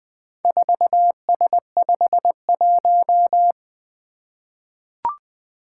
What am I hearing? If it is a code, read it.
Morse code, heard as a keyed tone, about 20 wpm, 700 Hz. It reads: 4S51
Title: Four Letter-Number Combinations - 20wpm